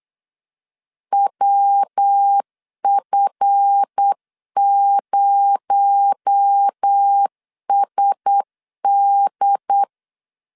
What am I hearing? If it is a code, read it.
Morse code, heard as a keyed tone, about 8 wpm, 800 Hz. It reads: WF0SD
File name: LEZIONI / 8 YJ45 / 8 YJ4 8wpm 3246